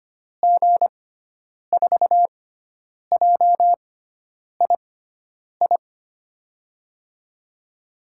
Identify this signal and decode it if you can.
Morse code, heard as a keyed tone, about 25 wpm, spaced as for 9 wpm, 700 Hz. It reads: G4JII
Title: USA Call Signs - 6x Character Spacing - 25wpm